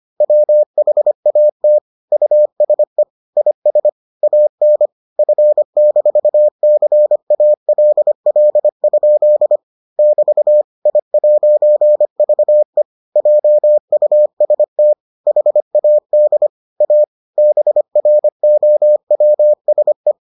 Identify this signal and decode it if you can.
Morse code, heard as a keyed tone, about 25 wpm, 600 Hz. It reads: WHAT USE IS AN F-CALL? <BT> I'VE JUST HAD A BROWSE